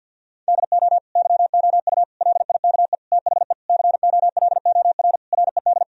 Morse code, heard as a keyed tone, about 52 wpm, 700 Hz. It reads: DO YOU LIKE THE COLOR RED
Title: Sentences from Top 1000 Words - Review Part 2 - 50wpm